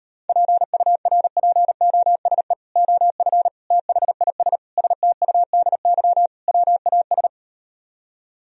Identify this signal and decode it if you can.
Morse code, heard as a keyed tone, about 38 wpm, 700 Hz. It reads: PURPOSE OF THIS STUDY WAS